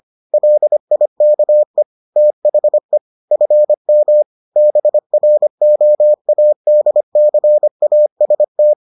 Morse code, heard as a keyed tone, about 25 wpm, 600 Hz. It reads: LIKE THE FM BROADCAST